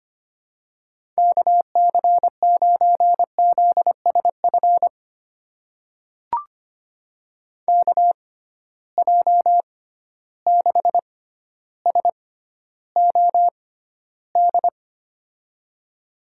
Morse code, heard as a keyed tone, about 25 wpm, spaced as for 13 wpm, 700 Hz. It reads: KC9ZSF KJ6SOD
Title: USA Call Signs - 6 Characters - 6x Character Spacing - 25wpm